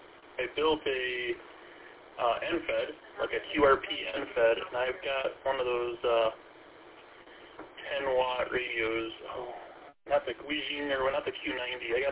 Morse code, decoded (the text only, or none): none